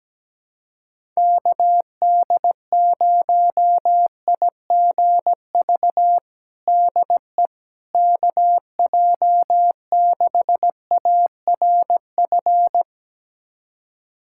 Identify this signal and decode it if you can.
Morse code, heard as a keyed tone, about 17 wpm, 700 Hz. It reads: KD0IGV DE KJ6ARF
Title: USA Callsign DE Callsign - 6 Characters - 17wpm